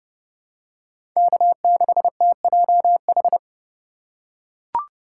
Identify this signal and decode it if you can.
Morse code, heard as a keyed tone, about 30 wpm, 700 Hz. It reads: K6TJH
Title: USA Call Signs - 5 Characters - 2x Character Spacing - 30wpm